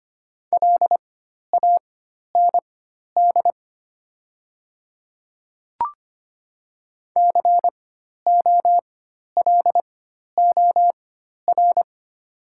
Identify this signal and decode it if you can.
Morse code, heard as a keyed tone, about 25 wpm, spaced as for 12 wpm, 700 Hz. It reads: LAND COLOR
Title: 4x Character Spacing - Top 300 Words - 25wpm